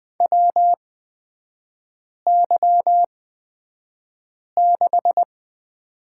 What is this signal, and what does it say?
Morse code, heard as a keyed tone, about 20 wpm, spaced as for 5 wpm, 690 Hz. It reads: WY6
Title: 236 Lesson 38e - Letter q - Three Characters 20wpm